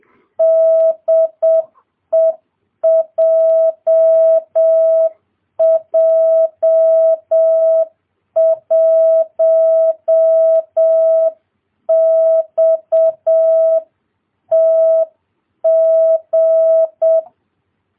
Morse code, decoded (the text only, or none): DE JJ1XTG